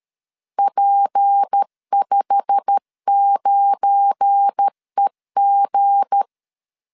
Morse code, heard as a keyed tone, about 13 wpm, 800 Hz. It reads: P59EG
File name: CW / DET / 1 9 1500C S C 12wpm 3811